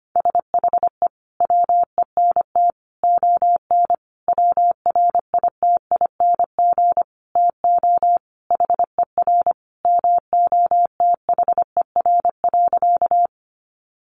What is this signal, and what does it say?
Morse code, heard as a keyed tone, about 25 wpm, 700 Hz. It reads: SHE WENT ON WRITING TO HER MOTHER.
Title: Sentences from Top 800 Words 25wpm